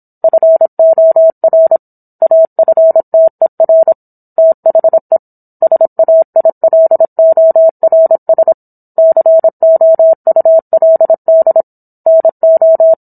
Morse code, heard as a keyed tone, about 27 wpm, 650 Hz. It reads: FOR AFTER THE SAILORS COULD NO